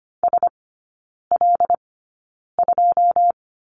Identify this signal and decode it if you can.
Morse code, heard as a keyed tone, about 25 wpm, spaced as for 9 wpm, 690 Hz. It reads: SL2
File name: Lesson 14e - Number 2 - Three Characters 25wpm